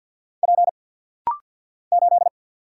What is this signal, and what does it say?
Morse code, heard as a keyed tone, about 50 wpm, 690 Hz. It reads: P 8
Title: Single Letter-Number Alt - Rapid-Fire 50wpm